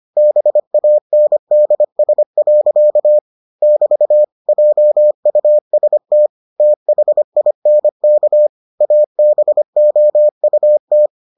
Morse code, heard as a keyed tone, about 25 wpm, 600 Hz. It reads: BANDS. <BT> JUST THINK ABOUT